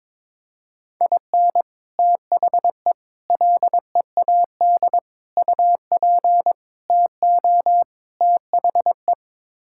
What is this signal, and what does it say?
Morse code, heard as a keyed tone, about 22 wpm, 700 Hz. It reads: IN THE LEAD UP TO THE
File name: practice / 22wpm / Sets of 6 Words - 22wpm